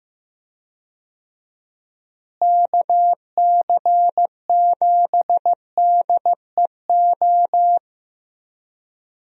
Morse code, heard as a keyed tone, about 15 wpm, 700 Hz. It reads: KC7DEO
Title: USA Call Signs - 6 Characters - Repeated 2x - 15wpm